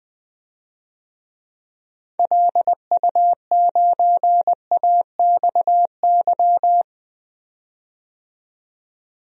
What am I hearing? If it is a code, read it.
Morse code, heard as a keyed tone, about 20 wpm, 700 Hz. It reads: LU9AXY
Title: DXCC-100 Call Signs - 6 Characters - 20wpm